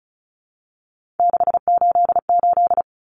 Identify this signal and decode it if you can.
Morse code, heard as a keyed tone, about 34 wpm, 700 Hz. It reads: 688
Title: Numbers - 3-Digit 35wpm